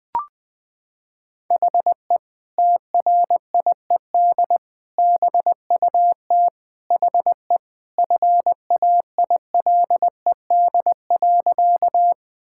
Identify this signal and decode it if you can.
Morse code, heard as a keyed tone, about 20 wpm, 700 Hz. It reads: HE TRIED BUT HE FAILED.
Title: Sentences with 2-Word Phrases - 20wpm